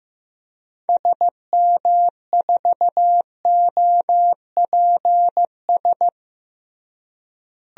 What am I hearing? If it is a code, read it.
Morse code, heard as a keyed tone, about 15 wpm, 700 Hz. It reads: SM4OPS
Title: DXCC-100 Call Signs - 6 Characters - Repeated 3x - 15wpm